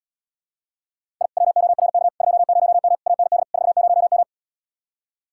Morse code, heard as a keyed tone, about 78 wpm, 700 Hz. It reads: I WORK 20M AND 40M